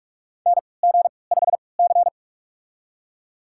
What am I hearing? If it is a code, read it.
Morse code, heard as a keyed tone, about 46 wpm, spaced as for 26 wpm, 690 Hz. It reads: NG5C
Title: USA Call Signs - 4 Characters - 3x Character Spacing - 45wpm